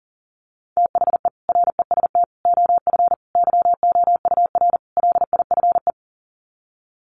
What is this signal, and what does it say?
Morse code, heard as a keyed tone, about 40 wpm, 700 Hz. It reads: THE REST OF YOUR LIFE
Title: Sets of 5 Words 40wpm